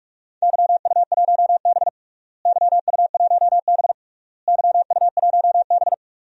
Morse code, heard as a keyed tone, about 46 wpm, 700 Hz. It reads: YU1B YU1B YU1B
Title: DXCC-100 Call Signs - 4 Characters - Repeated 3x - 45wpm